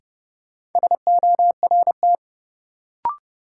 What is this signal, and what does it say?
Morse code, heard as a keyed tone, about 31 wpm, 700 Hz. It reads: SORT
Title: Sentences from Top 1000 Words - Review Part 3 - New Word Drill - 30wpm